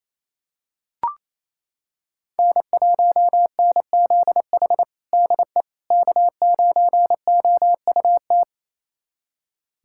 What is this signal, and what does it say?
Morse code, heard as a keyed tone, about 28 wpm, 700 Hz. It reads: N1NZH DE K9OUT
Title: USA Callsign DE Callsign - 5 Characters - 28wpm